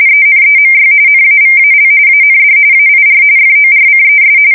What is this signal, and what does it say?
RTTY, about 45.45 baud. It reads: ENEMY NAVAL AND AIR FORCES
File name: fsk170 / USS-New-Jersey